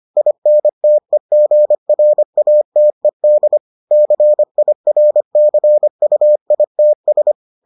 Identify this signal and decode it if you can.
Morse code, heard as a keyed tone, about 25 wpm, 600 Hz. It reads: INTEGRATED CIRCUITS